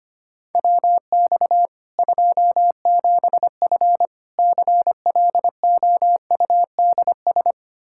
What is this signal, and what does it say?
Morse code, heard as a keyed tone, about 25 wpm, 700 Hz. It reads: WX 27F CLOUDS
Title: K1USN SST QSO Practice 25wpm